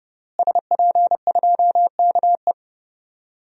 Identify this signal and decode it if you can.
Morse code, heard as a keyed tone, about 32 wpm, 700 Hz. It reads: SP2KE
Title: DXCC-100 Call Signs - 5 Characters - Repeated 3x - 30wpm